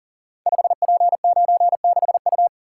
Morse code, heard as a keyed tone, about 39 wpm, 690 Hz. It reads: 5P96U
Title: Code Groups - 5 Letter:Numbers - 40wpm